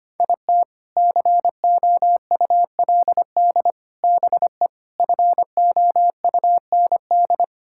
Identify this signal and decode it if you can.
Morse code, heard as a keyed tone, about 25 wpm, 700 Hz. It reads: IT COULD BE FOUND
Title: Sentences from Top 1000 Words - Review Part 5 - 25wpm